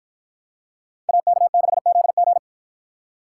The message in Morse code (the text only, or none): AK6ZG